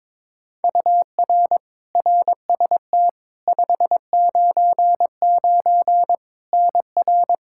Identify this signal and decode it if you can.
Morse code, heard as a keyed tone, about 23 wpm, 700 Hz. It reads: UR RST 599 NR
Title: Practice QSO - Exchange 1 - Beginning - 22wpm